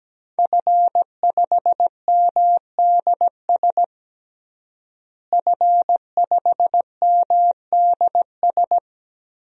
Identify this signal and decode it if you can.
Morse code, heard as a keyed tone, about 17 wpm, 700 Hz. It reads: F5MDS F5MDS
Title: DXCC-100 Call Signs - 5 Characters - Repeated 3x - 17wpm